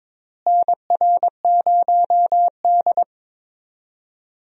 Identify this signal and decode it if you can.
Morse code, heard as a keyed tone, about 22 wpm, 700 Hz. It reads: NR0D
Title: USA Call Signs - 4 Characters - Repeated 3x - 22wpm